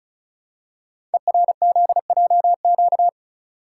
Morse code, heard as a keyed tone, about 36 wpm, 690 Hz. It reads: ERZJQ
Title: Code Groups - 5 Letters - 35wpm